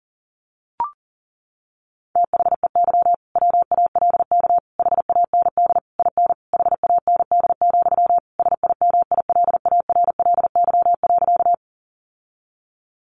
Morse code, heard as a keyed tone, about 40 wpm, 700 Hz. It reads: THEY WALK HAND IN HAND, SIMILARLY.
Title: Sentences from Top 1400 Words 40wpm